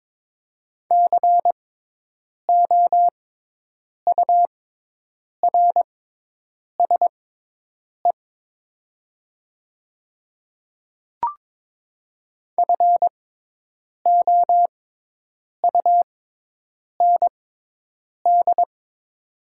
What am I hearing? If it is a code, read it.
Morse code, heard as a keyed tone, about 22 wpm, spaced as for 8 wpm, 700 Hz. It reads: COURSE FOUND
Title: 6x Character Spacing - Top 500 Words - 22wpm